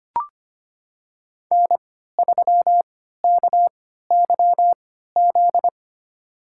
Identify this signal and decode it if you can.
Morse code, heard as a keyed tone, about 25 wpm, spaced as for 14 wpm, 700 Hz. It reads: N3KYZ
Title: USA Call Signs - 5 Characters - 3x Character Spacing - 25wpm